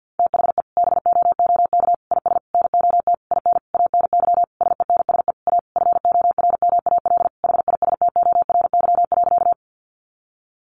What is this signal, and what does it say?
Morse code, heard as a keyed tone, about 48 wpm, 700 Hz. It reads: THE BOOK IS NOT IN ANY SENSE A FORMAL HISTORY.
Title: Sentences from Top 1300 Words 50wpm